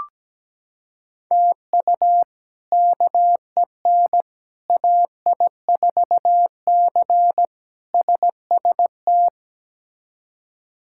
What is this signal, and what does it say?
Morse code, heard as a keyed tone, about 17 wpm, 700 Hz. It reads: TU KEN AI4C SST
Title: K1USN SST QSO Practice - 17wpm